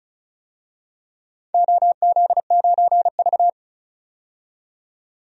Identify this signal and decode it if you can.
Morse code, heard as a keyed tone, about 36 wpm, 700 Hz. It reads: OZ9V
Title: DXCC-100 Call Signs - 4 Characters - 35wpm